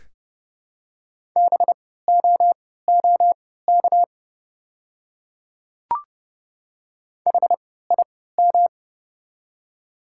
Morse code, heard as a keyed tone, about 30 wpm, spaced as for 17 wpm, 690 Hz. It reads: BOOK HIM
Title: 3x Character Spacing - Top 300 Words - 30wpm